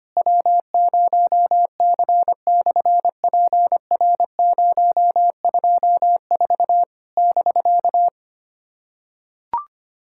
Morse code, heard as a keyed tone, about 24 wpm, 700 Hz. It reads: W0C/PR024 <BK>
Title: SOTA QSO Practice 25wpm